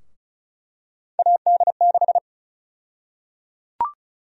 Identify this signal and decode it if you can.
Morse code, heard as a keyed tone, about 34 wpm, 700 Hz. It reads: AD6
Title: USA Call Sign Prefixes - 35wpm